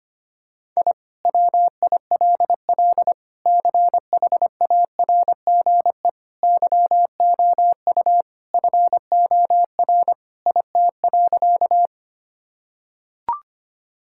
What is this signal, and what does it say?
Morse code, heard as a keyed tone, about 25 wpm, 700 Hz. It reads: I WILL CHARGE YOU FOR IT.
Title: Sentences from Top 1000 Words - Review Part 6 - 25wpm